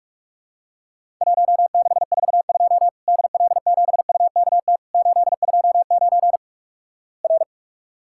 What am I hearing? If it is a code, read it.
Morse code, heard as a keyed tone, about 46 wpm, 700 Hz. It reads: 1642 DL7UKT 829 R
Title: Worked All Europe DX Contest - QTC Exchange - 45wpm